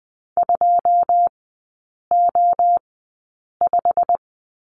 Morse code, heard as a keyed tone, about 20 wpm, spaced as for 8 wpm, 700 Hz. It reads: 2O5
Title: 121 Lesson 20e - Number 3 - Three Characters 20wpm